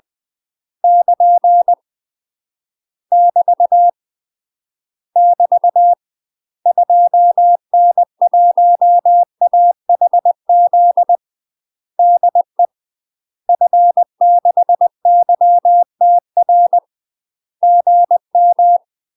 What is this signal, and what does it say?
Morse code, heard as a keyed tone, about 20 wpm, 700 Hz. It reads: <KN> <BT> <BT> 2N1AHZ DE F6YTR GM
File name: UK MASTER QSOs TRAINING / 20WPM / UK MASTER QSOs 0040